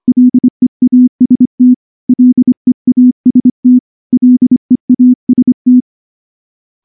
Morse code, heard as a keyed tone, about 24 wpm, 250 Hz. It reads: LEAST LEAST LEAST
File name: morse-training-25WPM